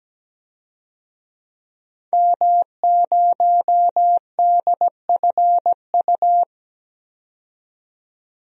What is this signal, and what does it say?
Morse code, heard as a keyed tone, about 17 wpm, 700 Hz. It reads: M0DFU
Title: DXCC-100 Call Signs - 5 Characters - 17wpm